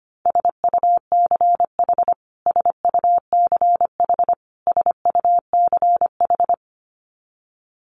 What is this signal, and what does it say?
Morse code, heard as a keyed tone, about 25 wpm, 700 Hz. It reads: SUCH SUCH SUCH
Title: Top 300 Words Repeated 25wpm